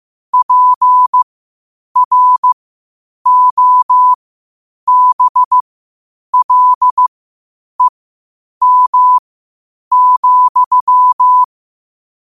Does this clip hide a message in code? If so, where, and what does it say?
Yes: Morse code from 0.3 to 11.4 s, heard as a keyed tone, about 15 wpm, spaced as for 8 wpm, 1000 Hz. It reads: PROBLEM,